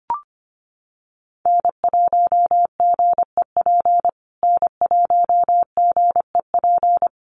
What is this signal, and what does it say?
Morse code, heard as a keyed tone, about 25 wpm, 700 Hz. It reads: N1GEP N1GEP
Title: USA Call Signs Repeated 3x 25wpm